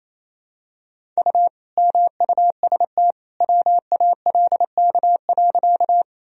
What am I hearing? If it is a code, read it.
Morse code, heard as a keyed tone, about 28 wpm, 700 Hz. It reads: U MUST WALK.
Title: Sentences from Top 300 Words and Abbreviations 28wpm